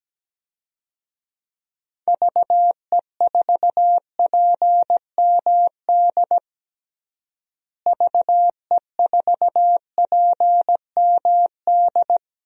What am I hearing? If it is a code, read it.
Morse code, heard as a keyed tone, about 17 wpm, 700 Hz. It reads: VE4PMD VE4PMD
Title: DXCC-100 Call Signs - 6 Characters - Repeated 3x - 17wpm